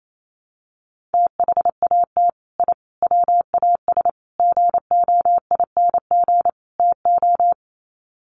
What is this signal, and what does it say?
Morse code, heard as a keyed tone, about 28 wpm, 700 Hz. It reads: THAT I WAS GOING TO
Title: Sets of 5 Words Encore 28wpm